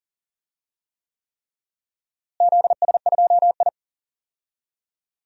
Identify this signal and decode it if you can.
Morse code, heard as a keyed tone, about 40 wpm, 700 Hz. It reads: ZS2I
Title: DXCC-100 Call Signs - 4 Characters - Repeated 3x - 40wpm